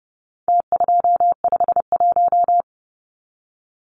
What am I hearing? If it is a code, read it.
Morse code, heard as a keyed tone, about 30 wpm, 700 Hz. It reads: T251
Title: Numbers - Contest Style - 4-Digit 30wpm